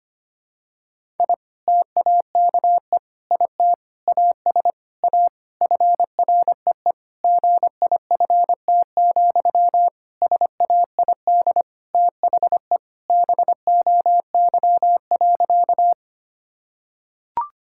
Morse code, heard as a keyed tone, about 25 wpm, 700 Hz. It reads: I TAKE IT AS A FREE GIFT, SAID THE BOY.
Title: Sentences from Top 1000 Words - Review Part 6 - 25wpm